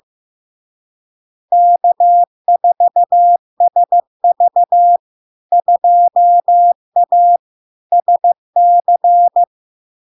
Morse code, heard as a keyed tone, about 15 wpm, 700 Hz. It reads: K4SV 2A SC
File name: LARK-FD-2022-Log-in-CW-15wpm-700-Hz